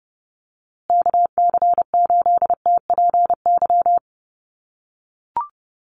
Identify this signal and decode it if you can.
Morse code, heard as a keyed tone, about 30 wpm, 700 Hz. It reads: KC8TPY
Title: USA Call Signs Repeated 3x 30wpm